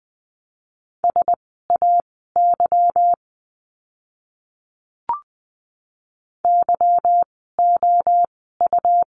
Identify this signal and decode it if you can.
Morse code, heard as a keyed tone, about 20 wpm, spaced as for 14 wpm, 700 Hz. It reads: SAY YOU
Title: 230 Lesson 37f - Letter x - Words 20wpm